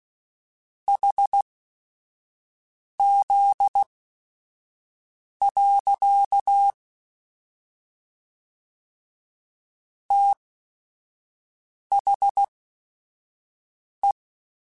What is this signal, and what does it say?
Morse code, heard as a keyed tone, about 16 wpm, spaced as for 5 wpm, 800 Hz. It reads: HZ. THE
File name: Technical Subjects / Technical Subjects-mp3s / Sw Pwr Sup @5